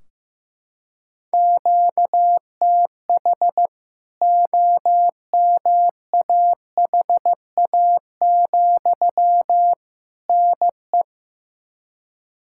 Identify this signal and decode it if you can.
Morse code, heard as a keyed tone, about 16 wpm, 700 Hz. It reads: QTH OMAHA, NE
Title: K1USN SST QSO Practice 15wpm